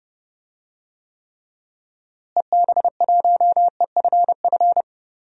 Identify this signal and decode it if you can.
Morse code, heard as a keyed tone, about 30 wpm, 700 Hz. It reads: EB1EFF
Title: DXCC-100 Call Signs - 6 Characters - Repeated 3x - 30wpm